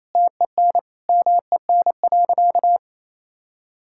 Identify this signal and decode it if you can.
Morse code, heard as a keyed tone, about 27 wpm, 700 Hz. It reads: TEN MEN.
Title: Sentences from Top 1000 Words - Review Part 6 - 28wpm